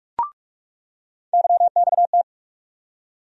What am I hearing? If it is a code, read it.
Morse code, heard as a keyed tone, about 44 wpm, 700 Hz. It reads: YXT
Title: USA Call Sign Postfixes - 3 Letters 45wpm